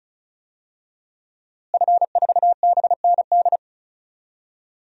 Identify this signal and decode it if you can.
Morse code, heard as a keyed tone, about 34 wpm, 700 Hz. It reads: F4BND
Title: DXCC-100 Call Signs - 5 Characters - Repeated 3x - 35wpm